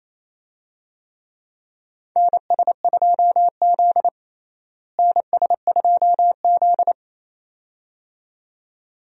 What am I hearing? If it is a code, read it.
Morse code, heard as a keyed tone, about 28 wpm, 700 Hz. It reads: NS2Z NS2Z
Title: USA Call Signs - 4 Characters - Repeated 2x - 28wpm